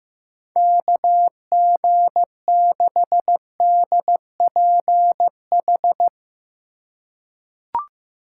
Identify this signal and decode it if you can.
Morse code, heard as a keyed tone, about 15 wpm, 700 Hz. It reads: KG6DPH
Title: USA Call Signs - 6 Characters - 3x Character Spacing - 15wpm